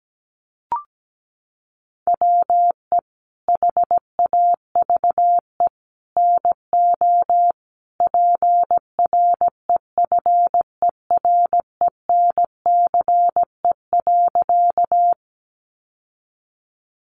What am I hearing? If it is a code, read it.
Morse code, heard as a keyed tone, about 17 wpm, 700 Hz. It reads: WE HAVE NO PREFERENCE.
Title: Sentences from Top 1500 Words 17wpm